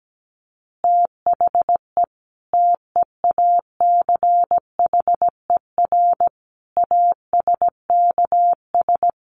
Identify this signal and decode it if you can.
Morse code, heard as a keyed tone, about 17 wpm, 700 Hz. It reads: THE TEACHER ASKS
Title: Additional Easy Phrases 17wpm